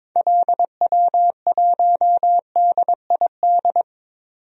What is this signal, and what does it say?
Morse code, heard as a keyed tone, about 22 wpm, 700 Hz. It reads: LW1DID
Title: DXCC-100 Call Signs - 6 Characters - 22wpm